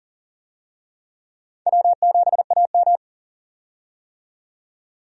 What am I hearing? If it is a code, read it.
Morse code, heard as a keyed tone, about 40 wpm, 700 Hz. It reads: W7AM
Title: USA Call Signs - 4 Characters - 40wpm